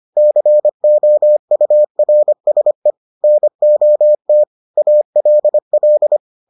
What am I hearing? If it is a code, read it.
Morse code, heard as a keyed tone, about 25 wpm, 600 Hz. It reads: COURSE NOT ALL